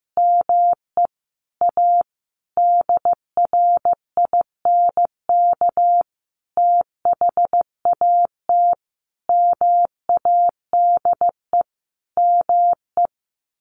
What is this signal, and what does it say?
Morse code, heard as a keyed tone, about 15 wpm, 700 Hz. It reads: ME A DRINK THAT MADE ME
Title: Sentences from Top 1500 Words 15wpm